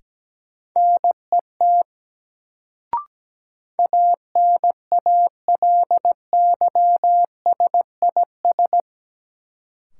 Morse code, heard as a keyed tone, about 17 wpm, 700 Hz. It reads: NET ANALYSIS
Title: Sentences from Top 1400 Words - New Word Drill Alt - 17wpm